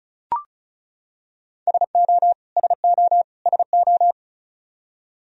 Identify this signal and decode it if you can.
Morse code, heard as a keyed tone, about 36 wpm, 700 Hz. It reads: SO SO SO
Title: Top 100 Words Repeated 35wpm